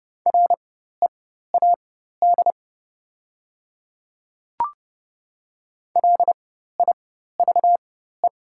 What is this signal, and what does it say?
Morse code, heard as a keyed tone, about 30 wpm, spaced as for 14 wpm, 700 Hz. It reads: READ LIVE
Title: 4x Character Spacing - Top 300 Words - 30wpm